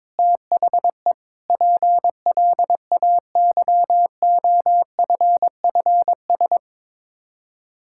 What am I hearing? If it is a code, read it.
Morse code, heard as a keyed tone, about 22 wpm, 700 Hz. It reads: THE PLAYOFFS
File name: practice / 22wpm / Sets of 7 Words - 22wpm